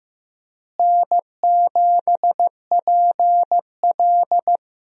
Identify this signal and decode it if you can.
Morse code, heard as a keyed tone, about 15 wpm, 700 Hz. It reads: N7PL
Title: USA Call Signs - 4 Characters - Repeated 3x - 15wpm